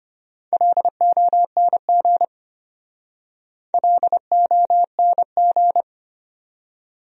Morse code, heard as a keyed tone, about 33 wpm, 700 Hz. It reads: LONG LONG